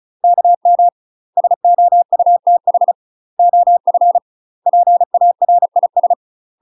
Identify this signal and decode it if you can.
Morse code, heard as a keyed tone, about 34 wpm, 700 Hz. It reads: KM SOUTH OF PARIS
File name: UK MASTER QSOs TRAINING / 35WPM / UK MASTER QSOs 0021